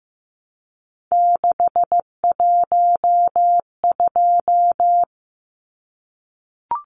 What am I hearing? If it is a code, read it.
Morse code, heard as a keyed tone, about 15 wpm, 700 Hz. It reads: 612
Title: Numbers - 3-Digit 15wpm